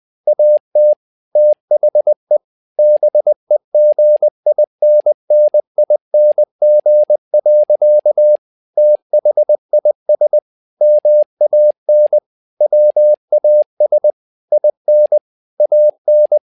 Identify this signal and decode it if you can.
Morse code, heard as a keyed tone, about 20 wpm, 600 Hz. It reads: AT THE BEGINNING. THIS MAN WAS IN AN
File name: learning center / tower trouble / tower trouble 20wpm